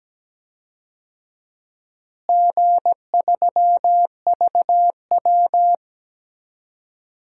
Morse code, heard as a keyed tone, about 17 wpm, 700 Hz. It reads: G3VW
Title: DXCC-100 Call Signs - 4 Characters - Repeated 2x - 17wpm